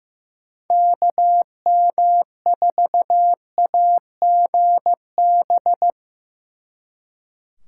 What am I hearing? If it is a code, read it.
Morse code, heard as a keyed tone, about 16 wpm, 700 Hz. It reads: KM4AGB
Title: Speed Racing - USA 2x3 Call Signs 15wpm